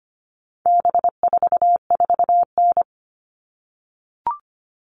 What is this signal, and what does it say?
Morse code, heard as a keyed tone, about 25 wpm, 700 Hz. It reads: B44N
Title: All Cut Numbers - 4-Digit 25wpm